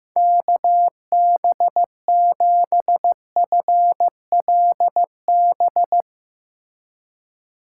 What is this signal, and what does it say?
Morse code, heard as a keyed tone, about 16 wpm, 700 Hz. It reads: KB7FLB
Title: USA Call Signs - 6 Characters - 2x Character Spacing - 15wpm